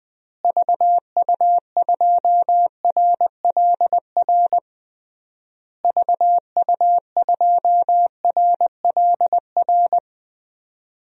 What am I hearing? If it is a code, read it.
Morse code, heard as a keyed tone, about 20 wpm, 700 Hz. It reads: VU2RLR VU2RLR
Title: DXCC-100 Call Signs - 6 Characters - Repeated 3x - 20wpm